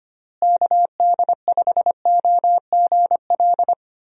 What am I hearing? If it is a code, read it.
Morse code, heard as a keyed tone, about 25 wpm, 700 Hz. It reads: KD5OGL
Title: USA Call Signs - 6 Characters - Repeated 3x - 25wpm